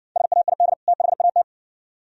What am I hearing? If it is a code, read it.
Morse code, heard as a keyed tone, about 56 wpm, 700 Hz. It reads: SAID THAT